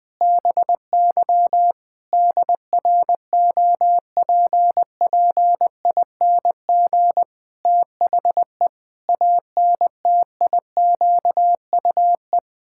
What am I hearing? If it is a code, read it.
Morse code, heard as a keyed tone, about 20 wpm, 700 Hz. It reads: BY DROPPING THE ANTIQUE
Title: Pangram Challenge v2 - 20wpm